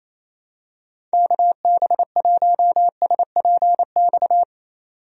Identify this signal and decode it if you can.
Morse code, heard as a keyed tone, about 28 wpm, 700 Hz. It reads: KB1SPX
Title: USA Call Signs - 6 Characters - 28wpm